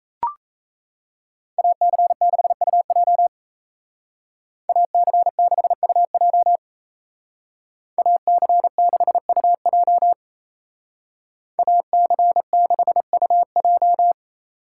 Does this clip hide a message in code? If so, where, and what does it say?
Yes: Morse code from 1.6 to 14.1 s, heard as a keyed tone, about 37 wpm, 700 Hz. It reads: AC6UJ AC6UJ AC6UJ AC6UJ